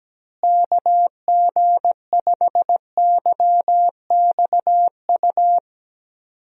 Morse code, heard as a keyed tone, about 17 wpm, 700 Hz. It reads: KG5YXU
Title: USA Call Signs - 6 Characters - Repeated 3x - 17wpm